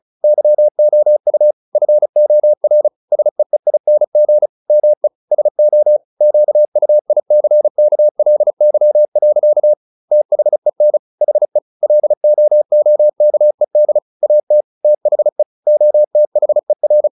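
Morse code, heard as a keyed tone, about 36 wpm, 600 Hz. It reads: YOU FOR SEEING ME SO QUICKLY. THEN HE LOOKED AT THE OTHER
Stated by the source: Morse practice at 35 wpm from CW med Gnister, sending Ken Follett's novel Never in English.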